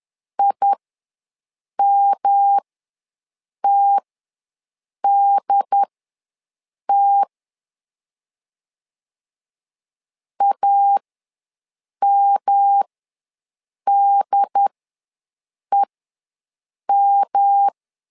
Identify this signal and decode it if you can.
Morse code, heard as a keyed tone, about 10 wpm, spaced as for 6 wpm, 800 Hz. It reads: IMTDT AMDEM
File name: LEZIONI / 3 MWRO / 3 M ws 10wpm 2121